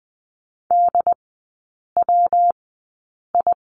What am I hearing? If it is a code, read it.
Morse code, heard as a keyed tone, about 20 wpm, spaced as for 8 wpm, 700 Hz. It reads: DWI
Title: 114 Lesson 19e - Letter w - Three Characters 20wpm